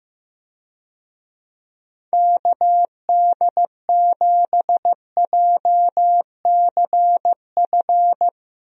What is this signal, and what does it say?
Morse code, heard as a keyed tone, about 15 wpm, 700 Hz. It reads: KD7JCF
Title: USA Call Signs - 6 Characters - Repeated 2x - 15wpm